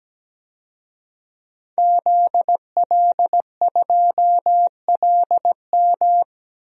Morse code, heard as a keyed tone, about 17 wpm, 700 Hz. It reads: ZL2LM
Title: DXCC-100 Call Signs - 5 Characters - 17wpm